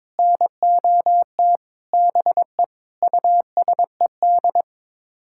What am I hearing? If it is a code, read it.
Morse code, heard as a keyed tone, about 21 wpm, 700 Hz. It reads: NOT BE USED
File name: Sets of 9 Words - 22wpm